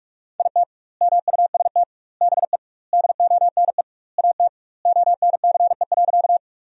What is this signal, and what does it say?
Morse code, heard as a keyed tone, about 46 wpm, 700 Hz. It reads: IT MUST BE DONE AT ONCE.
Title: 2x Word-Spacing - Sentences from Top 500 Words - 45wpm